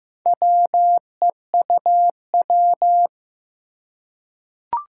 Morse code, heard as a keyed tone, about 15 wpm, 700 Hz. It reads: WEUW
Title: Four Challenging Letter-Number Combinations 15wpm